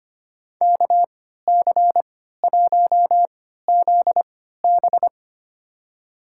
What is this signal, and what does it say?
Morse code, heard as a keyed tone, about 25 wpm, spaced as for 14 wpm, 700 Hz. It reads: KC1ZB
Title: USA Call Signs - 5 Characters - 3x Character Spacing - 25wpm